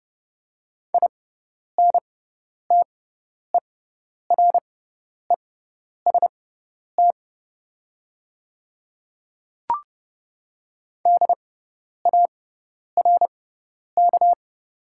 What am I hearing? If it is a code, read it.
Morse code, heard as a keyed tone, about 31 wpm, spaced as for 10 wpm, 690 Hz. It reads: INTEREST DARK
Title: 6x Character Spacing - Top 500 Words - 30wpm